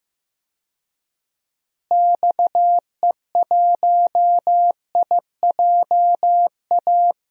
Morse code, heard as a keyed tone, about 15 wpm, 700 Hz. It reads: XE1IJA
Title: DXCC-100 Call Signs - 6 Characters - Repeated 2x - 15wpm